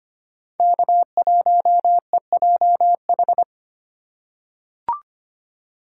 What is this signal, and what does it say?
Morse code, heard as a keyed tone, about 25 wpm, 700 Hz. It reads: K1EJH
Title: USA Call Signs - 6x Character Spacing - 25wpm